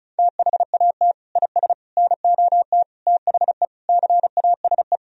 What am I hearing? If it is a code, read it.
Morse code, heard as a keyed tone, about 36 wpm, 700 Hz. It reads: THAT IS NOT THE CASE
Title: Sentences from Top 1000 Words - Review Part 2 - 35wpm